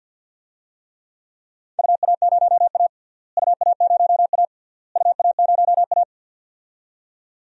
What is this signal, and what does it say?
Morse code, heard as a keyed tone, about 48 wpm, 700 Hz. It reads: UA0A UA0A UA0A
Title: DXCC-100 Call Signs - 4 Characters - Repeated 3x - 50wpm